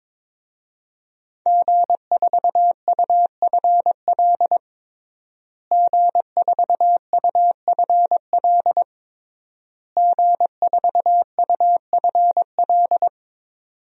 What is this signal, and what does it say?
Morse code, heard as a keyed tone, about 22 wpm, 700 Hz. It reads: G4UFL G4UFL G4UFL
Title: DXCC-100 Call Signs - 5 Characters - Repeated 3x - 22wpm